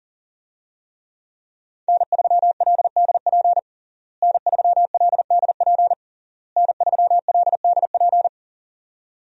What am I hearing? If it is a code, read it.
Morse code, heard as a keyed tone, about 40 wpm, 700 Hz. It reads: N3LDP N3LDP N3LDP
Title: USA Call Signs - 5 Characters - Repeated 3x - 40wpm